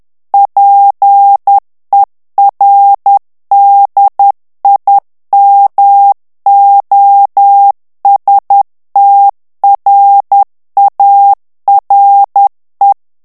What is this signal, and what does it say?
Morse code, heard as a keyed tone, about 10 wpm, 800 Hz. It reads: PER DIMOSTRARE
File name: morse mp3 / ITALIEN texte clair 1 vitesse 10 mots minute